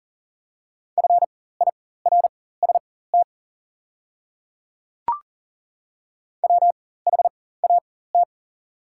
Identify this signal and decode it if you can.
Morse code, heard as a keyed tone, about 40 wpm, spaced as for 19 wpm, 700 Hz. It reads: FIRST WHAT
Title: 4x Character Spacing - Top 100 Words - 40wpm